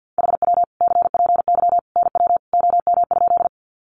Morse code, heard as a keyed tone, about 50 wpm, 700 Hz. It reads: HW CPY NW OM?
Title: Common QSO Phrases 50wpm